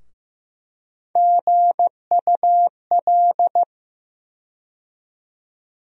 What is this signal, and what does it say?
Morse code, heard as a keyed tone, about 15 wpm, 700 Hz. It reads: GUL
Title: USA Call Sign Postfixes - 3 Letters 15wpm